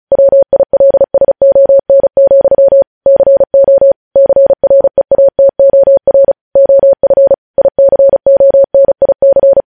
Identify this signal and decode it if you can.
Morse code, heard as a keyed tone, about 36 wpm, 550 Hz. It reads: WILSON, CO CREATOR OF ICONIC
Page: news headlines in morse code at 35 wpm